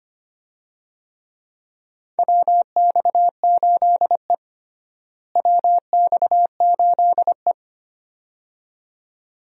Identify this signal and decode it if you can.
Morse code, heard as a keyed tone, about 25 wpm, 700 Hz. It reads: WX8E WX8E
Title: USA Call Signs - 4 Characters - Repeated 2x - 25wpm